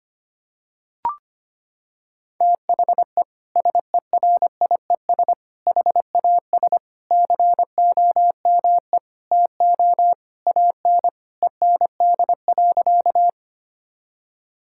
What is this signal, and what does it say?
Morse code, heard as a keyed tone, about 25 wpm, 700 Hz. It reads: THE SERIES HAS COME TO AN END.
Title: Sentences from Top 1000 Words - Review Part 5 - 25wpm